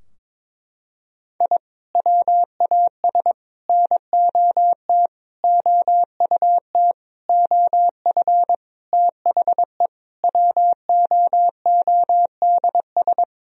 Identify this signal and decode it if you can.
Morse code, heard as a keyed tone, about 22 wpm, 700 Hz. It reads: I WAS NOT OUT OF THE WOODS
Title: Sentences from Top 1000 Words - Review Part 2 - 22wpm